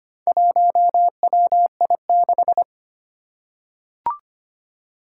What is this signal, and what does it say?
Morse code, heard as a keyed tone, about 25 wpm, 700 Hz. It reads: 1WI6
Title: Four Challenging Letter-Number Combinations 25wpm